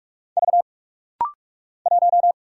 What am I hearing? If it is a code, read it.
Morse code, heard as a keyed tone, about 44 wpm, 700 Hz. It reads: V 1
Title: Single Letter-Number - Rapid-Fire 45wpm